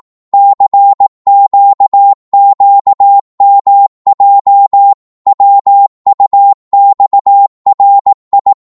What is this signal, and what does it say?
Morse code, heard as a keyed tone, about 18 wpm, 800 Hz. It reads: CQQMJ WUXRI
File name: Morse / ltr-18wpm